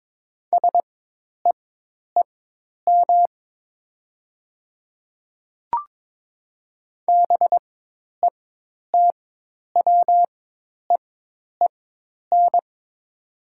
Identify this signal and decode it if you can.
Morse code, heard as a keyed tone, about 22 wpm, spaced as for 10 wpm, 700 Hz. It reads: SEEM BETWEEN
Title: 4x Character Spacing - Top 500 Words - 22wpm